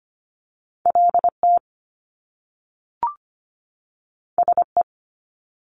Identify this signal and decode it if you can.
Morse code, heard as a keyed tone, about 25 wpm, 700 Hz. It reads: LT SE
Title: Top Two Letter Combinations in Words 25wpm